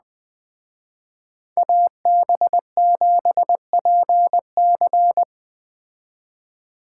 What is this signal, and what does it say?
Morse code, heard as a keyed tone, about 20 wpm, 700 Hz. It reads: AB7PC